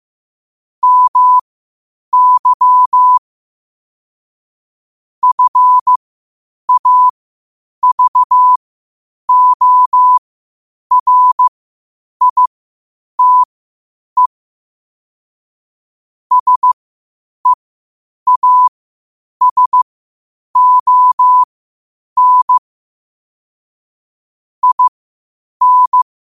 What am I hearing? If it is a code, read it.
Morse code, heard as a keyed tone, about 15 wpm, spaced as for 8 wpm, 1000 Hz. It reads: MY FAVORITE SEASON IN
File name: qotdfiles / 8wpm